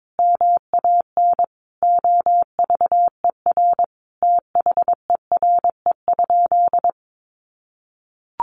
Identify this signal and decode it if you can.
Morse code, heard as a keyed tone, about 22 wpm, 700 Hz. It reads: MAN OVER THERE?
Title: Sentences from Top 200 Words 22wpm